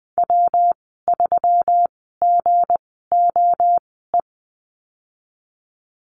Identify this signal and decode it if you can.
Morse code, heard as a keyed tone, about 20 wpm, spaced as for 15 wpm, 700 Hz. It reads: W3GOE
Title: 167 Lesson 27g - Number 7 - Call Signs 20wpm